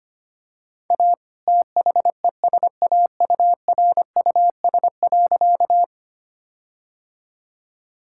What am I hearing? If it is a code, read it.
Morse code, heard as a keyed tone, about 25 wpm, 700 Hz. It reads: A THESAURUS.
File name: practice / 25wpm / All-Age Appropriate Jokes - 25wpm